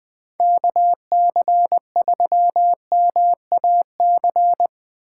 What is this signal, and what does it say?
Morse code, heard as a keyed tone, about 20 wpm, 700 Hz. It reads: KC3MAC
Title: USA Call Signs - 6 Characters - 2x Character Spacing - 20wpm